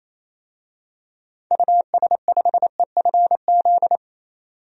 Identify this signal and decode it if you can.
Morse code, heard as a keyed tone, about 28 wpm, 700 Hz. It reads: US5EFZ
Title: DXCC-100 Call Signs - 6 Characters - 28wpm